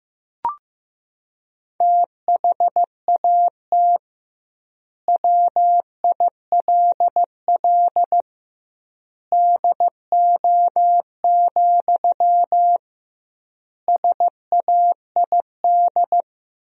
Morse code, heard as a keyed tone, about 16 wpm, 700 Hz. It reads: THAT WILL DO, SAID
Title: 2x Word-Spacing - Sentences from Top 100 Words - 15wpm